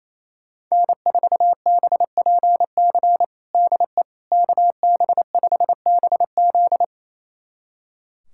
Morse code, heard as a keyed tone, about 28 wpm, 700 Hz. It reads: N4BPC DE KB5BZ
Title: USA Callsign DE Callsign - 5 Characters - 28wpm